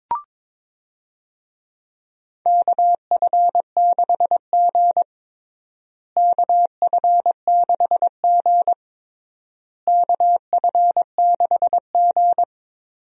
Morse code, heard as a keyed tone, about 22 wpm, 700 Hz. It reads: KF6G KF6G KF6G
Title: USA Call Signs - 4 Characters - Repeated 3x - 22wpm